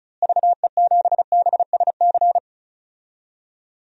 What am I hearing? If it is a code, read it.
Morse code, heard as a keyed tone, about 34 wpm, 700 Hz. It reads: VE7BSC
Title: DXCC-100 Call Signs - 6 Characters - Repeated 2x - 35wpm